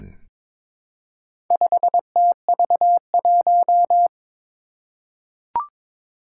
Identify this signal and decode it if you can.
Morse code, heard as a keyed tone, about 22 wpm, 700 Hz. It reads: 5TV1
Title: Four Challenging Letter-Number Combinations 22wpm